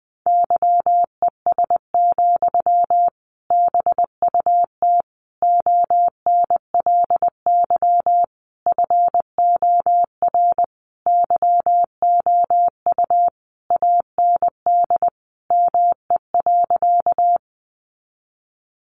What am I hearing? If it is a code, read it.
Morse code, heard as a keyed tone, about 20 wpm, 700 Hz. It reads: YES, BUT ONLY FOR YOU AND ME.
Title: Sentences from Top 500 Words 20wpm